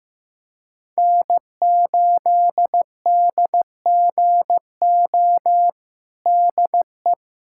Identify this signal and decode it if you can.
Morse code, heard as a keyed tone, about 15 wpm, 700 Hz. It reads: N8DGO DE
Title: USA Callsign DE Callsign - 5 Characters - 15wpm